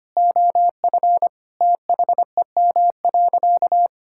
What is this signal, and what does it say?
Morse code, heard as a keyed tone, about 25 wpm, 700 Hz. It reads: OF THEM.
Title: Sentences from Top 1000 Words - Review Part 5 - 25wpm